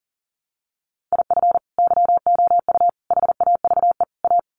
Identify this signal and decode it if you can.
Morse code, heard as a keyed tone, about 40 wpm, 700 Hz. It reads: IF YOU HAVE A
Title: Sets of 4 Words 40wpm